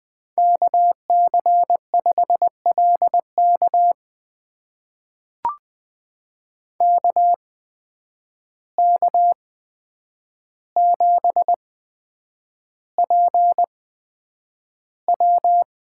Morse code, heard as a keyed tone, about 20 wpm, spaced as for 9 wpm, 700 Hz. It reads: KC5LK KK7PW
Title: USA Call Signs - 8x Character Spacing - 20wpm